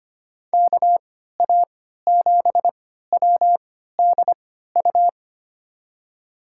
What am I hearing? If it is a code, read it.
Morse code, heard as a keyed tone, about 25 wpm, spaced as for 14 wpm, 700 Hz. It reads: KA7WDU
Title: USA Call Signs - 6 Characters - 3x Character Spacing - 25wpm